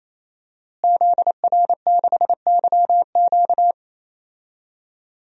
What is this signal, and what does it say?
Morse code, heard as a keyed tone, about 28 wpm, 700 Hz. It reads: ZR6YQ
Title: DXCC-100 Call Signs - 5 Characters - Repeated 3x - 28wpm